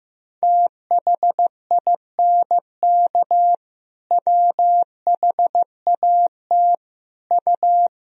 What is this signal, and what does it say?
Morse code, heard as a keyed tone, about 15 wpm, 700 Hz. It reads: THINK WHAT U
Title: Sentences from Top 200 Words and Abbreviations 15wpm